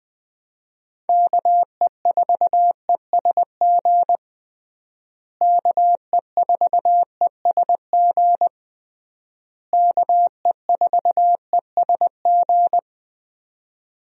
Morse code, heard as a keyed tone, about 20 wpm, 700 Hz. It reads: KE4ESG KE4ESG KE4ESG
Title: USA Call Signs - 6 Characters - Repeated 3x - 20wpm